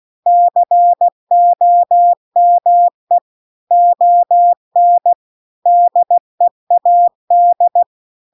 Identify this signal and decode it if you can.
Morse code, heard as a keyed tone, about 16 wpm, 700 Hz. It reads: COME ON DEAD
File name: CW-The-Big-Twoheaded-River-2-16WPM